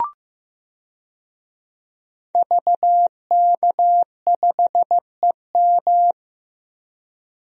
Morse code, heard as a keyed tone, about 15 wpm, 700 Hz. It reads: VK5EM
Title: DXCC-100 Call Signs - 5 Characters - 15wpm